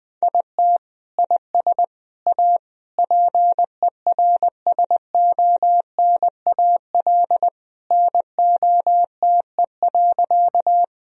Morse code, heard as a keyed tone, about 20 wpm, 700 Hz. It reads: IT IS A PERSONAL NOTE.
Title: Sentences from Top 1000 Words - Review Part 4 - 20wpm